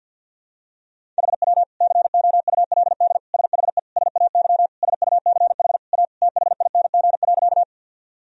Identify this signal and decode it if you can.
Morse code, heard as a keyed tone, about 52 wpm, 700 Hz. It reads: HW COULD SHE SAY SUCH A THING.
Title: Sentences from Top 200 Words and Abbreviations 50wpm